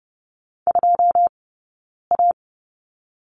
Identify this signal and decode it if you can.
Morse code, heard as a keyed tone, about 30 wpm, 700 Hz. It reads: 2 A
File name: 092 Lesson 16d - Letter c - Two Characters 30wpm